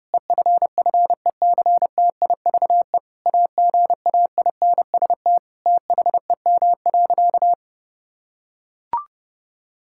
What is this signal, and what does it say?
Morse code, heard as a keyed tone, about 30 wpm, 700 Hz. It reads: EFFECTIVE AGAINST THEM.
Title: Sentences from Top 1000 Words - Review Part 6 - 30wpm